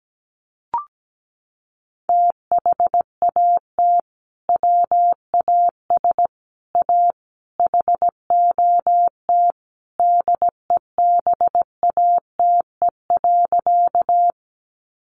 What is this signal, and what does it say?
Morse code, heard as a keyed tone, about 17 wpm, 700 Hz. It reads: THAT WAS A HOT DEBATE.
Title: Sentences from Top 1100 Words 17wpm